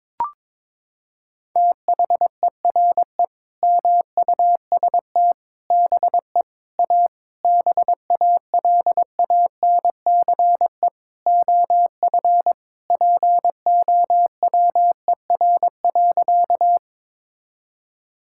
Morse code, heard as a keyed tone, about 22 wpm, 700 Hz. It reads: THERE MUST BE A BALANCE OF POWER.
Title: Sentences from Top 1000 Words - Review Part 6 - 22wpm